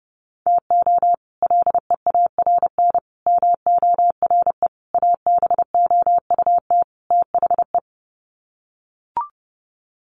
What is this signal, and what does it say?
Morse code, heard as a keyed tone, about 30 wpm, 700 Hz. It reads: TO LEARN MORE ABOUT THE
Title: Sets of 5 Words Encore2 30wpm